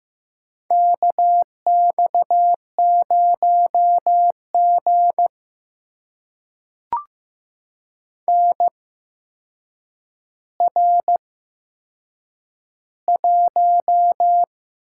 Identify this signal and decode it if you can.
Morse code, heard as a keyed tone, about 15 wpm, spaced as for 7 wpm, 700 Hz. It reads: KX0G NR1
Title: USA Call Signs - 4 Characters - 8x Character Spacing - 15wpm